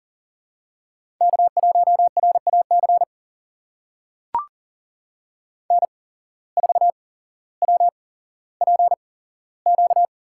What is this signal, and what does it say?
Morse code, heard as a keyed tone, about 40 wpm, spaced as for 17 wpm, 700 Hz. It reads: K1RAC N4WPQ
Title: USA Call Signs - 5 Characters - 8x Character Spacing - 40wpm